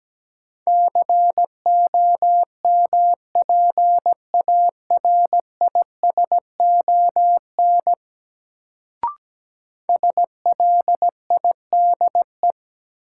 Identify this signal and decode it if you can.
Morse code, heard as a keyed tone, about 17 wpm, 700 Hz. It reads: COMPARISON SLIDE
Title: Sentences from Top 1600 Words - New Word Drill - 17wpm